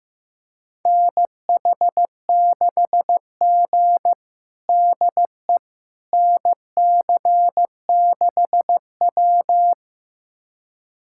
Morse code, heard as a keyed tone, about 15 wpm, 700 Hz. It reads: NH6G DE NC6W
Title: USA Callsign DE Callsign - 4 Characters - 15wpm